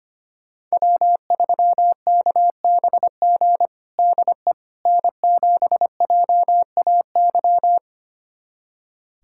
Morse code, heard as a keyed tone, about 25 wpm, 700 Hz. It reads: W3KBG DE N7JAY
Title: USA Callsign DE Callsign - 5 Characters - 25wpm